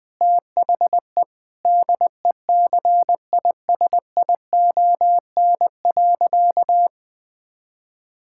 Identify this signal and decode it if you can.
Morse code, heard as a keyed tone, about 21 wpm, 700 Hz. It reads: THE DECISION.
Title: Sentences with 6-Word Phrases - Spoken Beginning - 20wpm